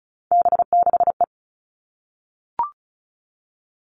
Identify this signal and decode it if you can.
Morse code, heard as a keyed tone, about 34 wpm, 700 Hz. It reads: B6E
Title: All Cut Numbers - 3-Digit 35wpm